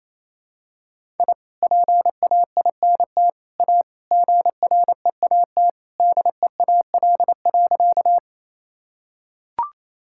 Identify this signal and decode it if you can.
Morse code, heard as a keyed tone, about 28 wpm, 700 Hz. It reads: I PAINT A GREAT DEAL.
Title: Sentences from Top 1000 Words - Review Part 6 - 28wpm